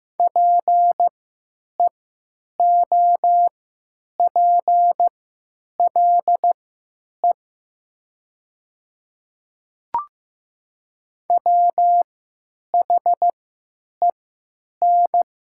Morse code, heard as a keyed tone, about 15 wpm, spaced as for 9 wpm, 690 Hz. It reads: PEOPLE WHEN
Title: 3x Character Spacing - Top 100 Words - 15wpm